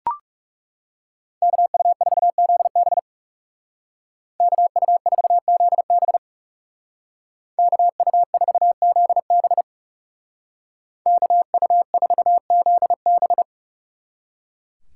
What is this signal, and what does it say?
Morse code, heard as a keyed tone, about 39 wpm, 700 Hz. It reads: KU4ZB KU4ZB KU4ZB KU4ZB